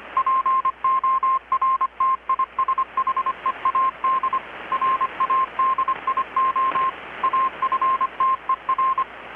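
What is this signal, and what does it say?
Morse code, heard as a keyed tone, about 25 wpm, 1060 Hz, with radio noise. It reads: PORTISHEAD RADIO AFTER
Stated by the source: Morse from KPH's Night of Nights XX maritime radiotelegraphy event, received in Chiapas, Mexico about 3,800 km away on a Kenwood R-600 with a grounded 10 m long-wire antenna.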